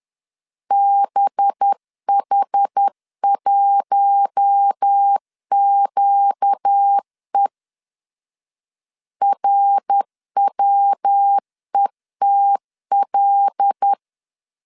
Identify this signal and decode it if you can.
Morse code, heard as a keyed tone, about 10 wpm, 800 Hz. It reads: BH1QE RWETL
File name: LEZIONI / 8 YJ45 / 8 Y 10wpm 2615